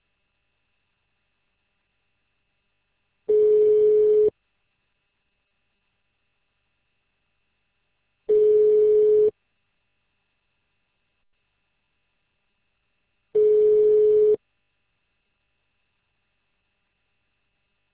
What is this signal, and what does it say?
Ringback tone (Europe, 425 Hz).